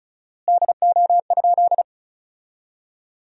Morse code, heard as a keyed tone, about 34 wpm, 700 Hz. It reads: DO?
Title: 4x Word-Spacing - Sentences from Top 100 Words - 35wpm